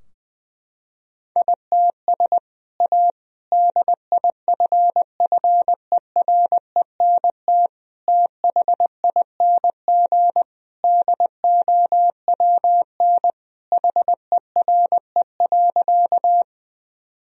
Morse code, heard as a keyed tone, about 21 wpm, 700 Hz. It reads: ITS A DIFFERENT THING DOWN HERE.
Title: Sentences from Top 1000 Words - Review Part 3 - 20wpm